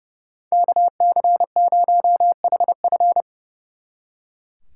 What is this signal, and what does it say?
Morse code, heard as a keyed tone, about 38 wpm, 700 Hz. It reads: KC0HF